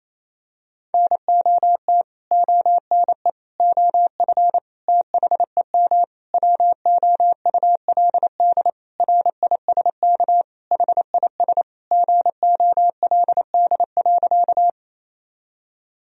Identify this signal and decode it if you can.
Morse code, heard as a keyed tone, about 28 wpm, 700 Hz. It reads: NOT ONE OF THEM WOULD RISK HIS GOLD.
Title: Sentences from Top 1000 Words - Review Part 4 - 28wpm